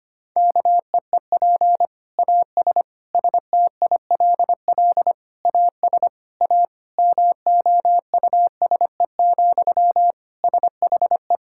Morse code, heard as a keyed tone, about 25 wpm, 700 Hz. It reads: KEEP AS STILL AS A MOUSE, SHE
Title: Sentences from Top 1000 Words - Review Part 5 - 25wpm